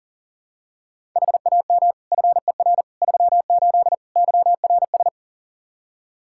Morse code, heard as a keyed tone, about 40 wpm, 700 Hz. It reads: HAM FER 38 YRS
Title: K1USN SST QSO Practice 40wpm